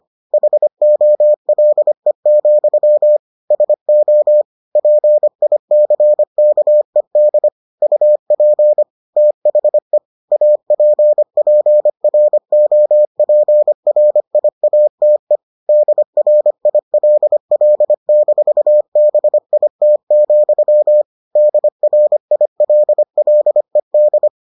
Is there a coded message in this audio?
Yes: Morse code from 0.3 to 24.3 s, heard as a keyed tone, about 25 wpm, 600 Hz. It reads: HOLE, SO PICKED UP THE APPROPRIATE DRILL-BIT, DRILLED